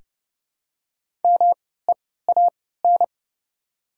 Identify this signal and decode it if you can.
Morse code, heard as a keyed tone, about 30 wpm, spaced as for 17 wpm, 700 Hz. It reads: MEAN
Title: 3x Character Spacing - Top 500 Words - 30wpm